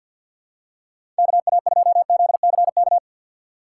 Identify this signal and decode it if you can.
Morse code, heard as a keyed tone, about 53 wpm, 700 Hz. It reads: KA2ZCK
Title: USA Call Signs - 6 Characters - 3x Character Spacing - 50wpm